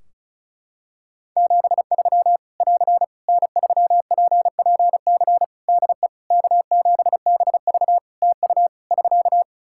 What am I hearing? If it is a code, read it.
Morse code, heard as a keyed tone, about 36 wpm, 700 Hz. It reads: 73 <AR> N3PPC DE K7BV TU <SK>
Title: Practice QSO - Exchange 4 - Ending - 35wpm